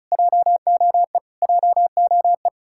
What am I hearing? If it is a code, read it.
Morse code, heard as a keyed tone, about 46 wpm, spaced as for 41 wpm, 700 Hz. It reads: JOE JOE
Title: Speed-Racing Common QSO Phrases - 35wpm